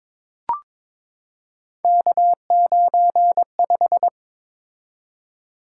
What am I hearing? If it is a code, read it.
Morse code, heard as a keyed tone, about 22 wpm, 700 Hz. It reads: K95
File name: practice / 22wpm / Three Letter-Number Combinations - 22wpm